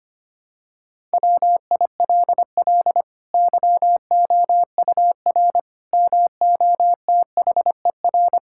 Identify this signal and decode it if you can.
Morse code, heard as a keyed tone, about 25 wpm, 700 Hz. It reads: WILL YOUR MOTHER
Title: Sentences from Top 1000 Words - Review Part 5 - 25wpm